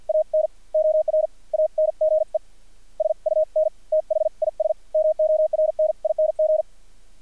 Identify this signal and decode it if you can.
Morse code, heard as a keyed tone, about 50 wpm, spaced as for 41 wpm, 620 Hz. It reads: AN ORANGE SUN THIS MORNING
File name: archived / CRSnet